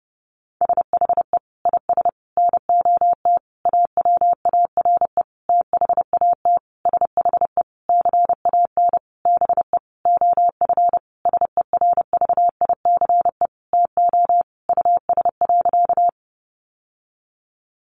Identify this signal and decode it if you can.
Morse code, heard as a keyed tone, about 30 wpm, 700 Hz. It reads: SHE IS NOT AWARE THAT SHE CAN BE OF SERVICE TO US.
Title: Sentences from Top 1200 Words 30wpm